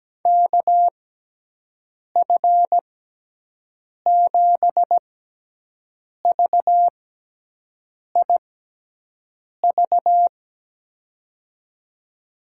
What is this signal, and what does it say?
Morse code, heard as a keyed tone, about 17 wpm, spaced as for 6 wpm, 700 Hz. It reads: KF7VIV
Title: USA Call Signs - 6 Characters - 6x Character Spacing - 17wpm